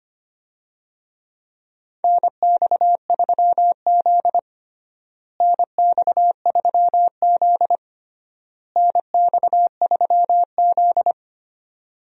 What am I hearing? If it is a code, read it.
Morse code, heard as a keyed tone, about 25 wpm, 700 Hz. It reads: NX3Z NX3Z NX3Z
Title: USA Call Signs - 4 Characters - Repeated 3x - 25wpm